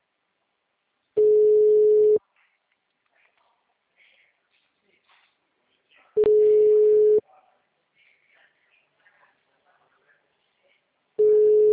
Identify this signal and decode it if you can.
Ringback tone (Europe, 425 Hz).
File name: out- -2018- - - .23735